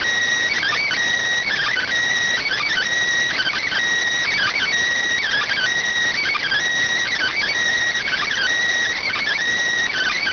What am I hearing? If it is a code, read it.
Emergency Alert System (SAME) tones (header bursts), heard as harsh buzzing bursts. It is heard from 1.1 to 9.3 s.